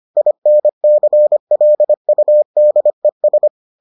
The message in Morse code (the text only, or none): INCLUDES